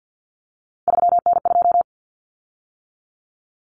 Morse code, heard as a keyed tone, about 48 wpm, 700 Hz. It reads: 3N2
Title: Common Cut Numbers - 3-Digit 50wpm